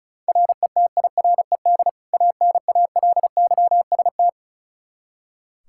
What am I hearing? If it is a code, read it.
Morse code, heard as a keyed tone, about 36 wpm, 700 Hz. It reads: RETIRED ANALYST
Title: Practice QSO - Exchange 3 - More Info - 35wpm